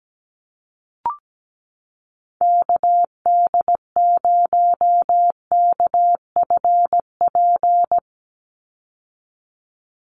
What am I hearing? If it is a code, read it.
Morse code, heard as a keyed tone, about 17 wpm, 700 Hz. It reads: KD0KFP
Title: USA Call Signs 17wpm